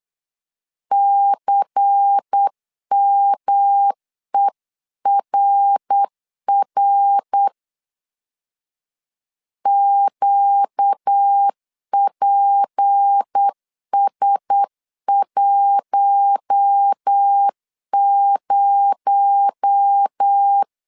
Morse code, heard as a keyed tone, about 8 wpm, 800 Hz. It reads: CMERR QPS10
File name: 8wpm